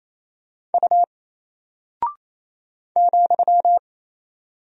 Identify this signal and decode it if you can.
Morse code, heard as a keyed tone, about 27 wpm, 700 Hz. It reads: U ,
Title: Single Character (Letter:Number:Punctuation:Prosign) Alt 28wpm